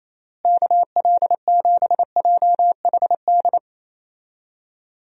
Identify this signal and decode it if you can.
Morse code, heard as a keyed tone, about 28 wpm, 700 Hz. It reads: KL7JHD
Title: USA Call Signs - 6 Characters - Repeated 3x - 28wpm